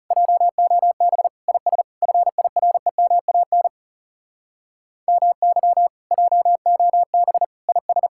Morse code, heard as a keyed tone, about 40 wpm, 700 Hz. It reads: JOB IS FIREMAN MY JOB IS